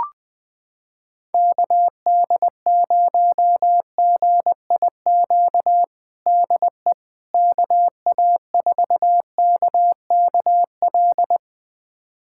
Morse code, heard as a keyed tone, about 20 wpm, 700 Hz. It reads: KD0GIQ DE KA4KKL
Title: USA Callsign DE Callsign - 6 Characters - 20wpm